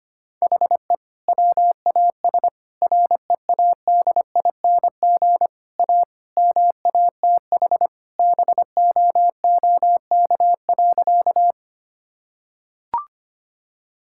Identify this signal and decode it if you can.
Morse code, heard as a keyed tone, about 25 wpm, 700 Hz. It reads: HE WAS READING A MATH BOOK.
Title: Sentences from Top 1000 Words - Review Part 6 - 25wpm